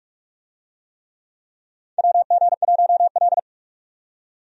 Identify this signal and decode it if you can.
Morse code, heard as a keyed tone, about 46 wpm, 700 Hz. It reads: WG1L
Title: USA Call Signs - 4 Characters - 45wpm